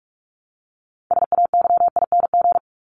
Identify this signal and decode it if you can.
Morse code, heard as a keyed tone, about 44 wpm, 700 Hz. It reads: SAYING
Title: ING Suffix - Top 100 Words with Base Words 45wpm